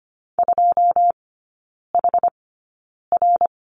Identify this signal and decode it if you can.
Morse code, heard as a keyed tone, about 25 wpm, spaced as for 9 wpm, 690 Hz. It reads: 2HR
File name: Lesson 14e - Number 2 - Three Characters 25wpm